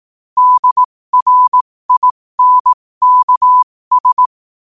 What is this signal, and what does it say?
Morse code, heard as a keyed tone, about 18 wpm, spaced as for 15 wpm, 1000 Hz. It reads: DRINKS
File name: qotdfiles / 15wpm